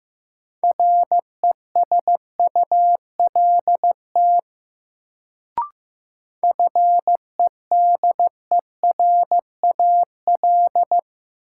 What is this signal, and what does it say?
Morse code, heard as a keyed tone, about 15 wpm, 700 Hz. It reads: RESULT FEDERAL
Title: Sets of 3 Words - New Word Drill - 15wpm